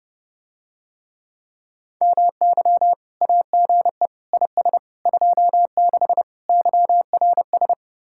Morse code, heard as a keyed tone, about 30 wpm, 700 Hz. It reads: MY AGE IS 26 YRS